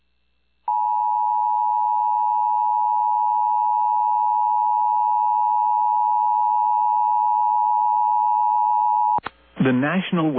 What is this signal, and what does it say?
Emergency Alert System (SAME) tones (the 853 and 960 Hz attention tone), with speech. It is heard from 0.6 to 9.2 s.
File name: eas / KUSO / KUSO